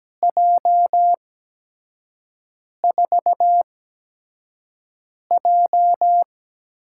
Morse code, heard as a keyed tone, about 17 wpm, spaced as for 5 wpm, 700 Hz. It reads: J4J
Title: USA Call Signs - 4 Characters - 8x Character Spacing - 17wpm